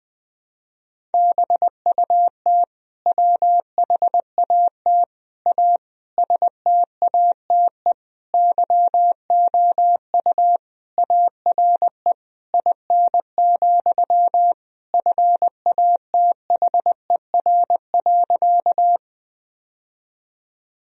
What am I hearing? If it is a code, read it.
Morse code, heard as a keyed tone, about 20 wpm, 700 Hz. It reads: BUT WHAT A STATE YOU ARE IN, FATHER.
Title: Sentences from Top 1000 Words - Review Part 2 - 20wpm